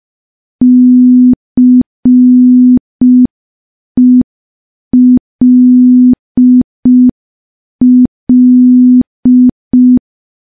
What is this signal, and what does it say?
Morse code, heard as a keyed tone, about 5 wpm, 250 Hz. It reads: CELL
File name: morse / morse-training-05WPM